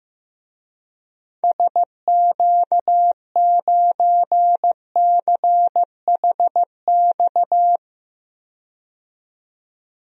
Morse code, heard as a keyed tone, about 15 wpm, 700 Hz. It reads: SQ9CHX
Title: DXCC-100 Call Signs - 6 Characters - 15wpm